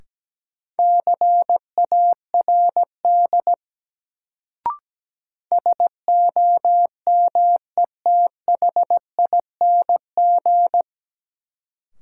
Morse code, heard as a keyed tone, about 17 wpm, 700 Hz. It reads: CARD SOMETHING
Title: Sentences from Top 600 Words - New Word Drill - 17wpm